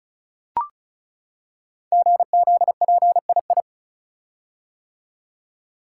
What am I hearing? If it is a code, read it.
Morse code, heard as a keyed tone, about 36 wpm, 700 Hz. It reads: GZPII
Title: Code Groups - 5 Letter:Numbers - 35wpm